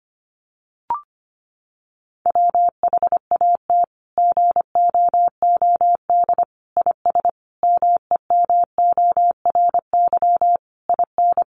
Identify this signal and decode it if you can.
Morse code, heard as a keyed tone, about 25 wpm, 700 Hz. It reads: WHAT GOOD IS MEMORY IN
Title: Sentences from Top 1300 Words 25wpm